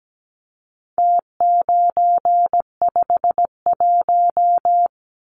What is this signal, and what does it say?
Morse code, heard as a keyed tone, about 17 wpm, 700 Hz. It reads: T951
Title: Numbers - Contest Style - 4-Digit 17wpm